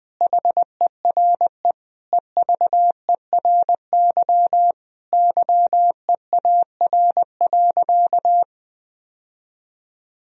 Morse code, heard as a keyed tone, about 20 wpm, 700 Hz. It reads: HERE EVERY YEAR.
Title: Sentences from Top 1000 Words - Review Part 1 - 20wpm